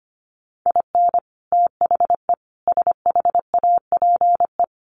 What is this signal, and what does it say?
Morse code, heard as a keyed tone, about 25 wpm, 700 Hz. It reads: IN THE SHAPE
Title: Sets of 5 Words Encore 25wpm